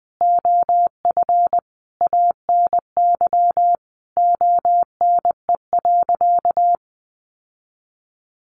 Morse code, heard as a keyed tone, about 20 wpm, 700 Hz. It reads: OF ANY ONE.
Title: Sentences from Top 1000 Words 20wpm